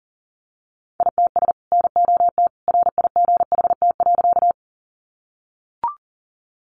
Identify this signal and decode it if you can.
Morse code, heard as a keyed tone, about 40 wpm, 700 Hz. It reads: ITS NOT RIGHT.
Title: Sentences from Top 600 Words 40wpm